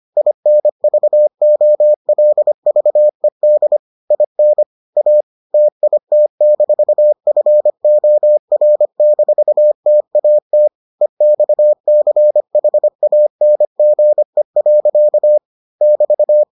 Morse code, heard as a keyed tone, about 25 wpm, 600 Hz. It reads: INVOLVED IN A TIT-FOR-TAT EXCHANGE. <BT>